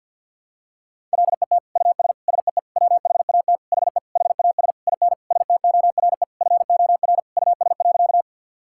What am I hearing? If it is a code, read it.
Morse code, heard as a keyed tone, about 51 wpm, 700 Hz. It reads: LET US SEE WHAT HE HAS IN STORE FOR US.
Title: Sentences from Top 1000 Words - Review Part 5 - 50wpm